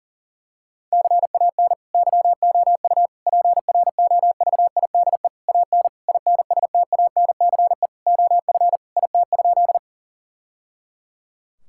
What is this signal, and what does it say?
Morse code, heard as a keyed tone, about 37 wpm, 700 Hz. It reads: CAN YOU PROVIDE AN INSTANCE OF IT?
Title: Sentences from Top 1000 Words - Review Part 5 - 40wpm